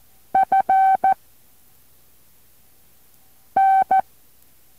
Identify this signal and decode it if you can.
Morse code, heard as a keyed tone, about 14 wpm, 770 Hz. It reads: F N
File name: intl morse code2